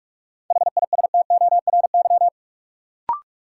Morse code, heard as a keyed tone, about 46 wpm, 700 Hz. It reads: HISTORY
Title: Sets of 3 Words - New Word Drill - 45wpm